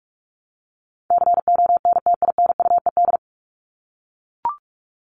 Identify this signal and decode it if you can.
Morse code, heard as a keyed tone, about 44 wpm, 700 Hz. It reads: CONTINUED
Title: ED Suffix - Top 100 Words with Base Words 45wpm